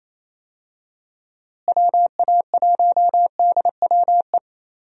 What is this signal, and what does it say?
Morse code, heard as a keyed tone, about 28 wpm, 700 Hz. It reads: WA1DWE
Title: USA Call Signs - 6 Characters - Repeated 3x - 28wpm